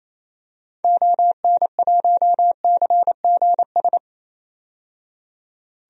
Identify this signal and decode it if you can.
Morse code, heard as a keyed tone, about 28 wpm, 700 Hz. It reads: ON1CGS
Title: DXCC-100 Call Signs - 6 Characters - 28wpm